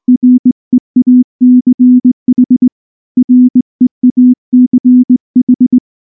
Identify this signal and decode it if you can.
Morse code, heard as a keyed tone, about 19 wpm, 250 Hz. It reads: REACH REACH
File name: morse / morse-training-20WPM